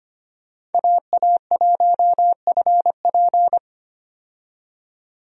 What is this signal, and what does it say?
Morse code, heard as a keyed tone, about 25 wpm, 700 Hz. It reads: AA1FP
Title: USA Call Signs - 5 Characters - 25wpm